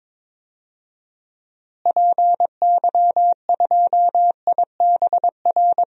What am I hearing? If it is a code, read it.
Morse code, heard as a keyed tone, about 22 wpm, 700 Hz. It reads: PY2IBR
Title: DXCC-100 Call Signs - 6 Characters - Repeated 2x - 22wpm